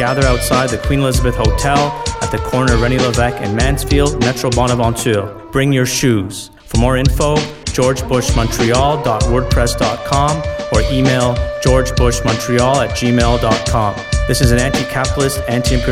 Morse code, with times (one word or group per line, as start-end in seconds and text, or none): none